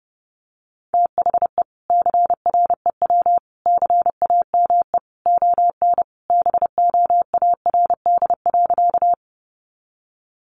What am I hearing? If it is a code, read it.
Morse code, heard as a keyed tone, about 30 wpm, 700 Hz. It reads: THE CREW CAME ON BOARD.
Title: Sentences from Top 1500 Words 30wpm